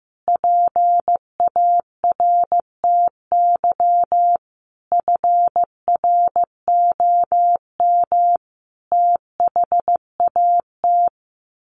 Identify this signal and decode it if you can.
Morse code, heard as a keyed tone, about 15 wpm, 700 Hz. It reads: PARTY FROM THAT
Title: Sentences from Top 1400 Words 15wpm